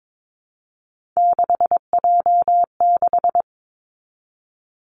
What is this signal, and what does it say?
Morse code, heard as a keyed tone, about 22 wpm, 700 Hz. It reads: 6J6
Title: Three Challenging Letter-Number Combinations 22wpm